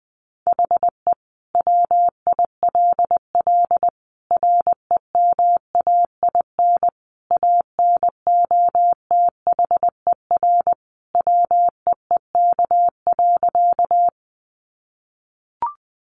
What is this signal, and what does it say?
Morse code, heard as a keyed tone, about 20 wpm, 700 Hz. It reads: HE WILL REMAIN ANOTHER WEEK.
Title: Sentences from Top 1200 Words 20wpm